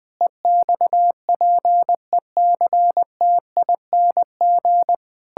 Morse code, heard as a keyed tone, about 20 wpm, 700 Hz. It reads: EXPECTING
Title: Sentences with 9-Word Phrases - 20wpm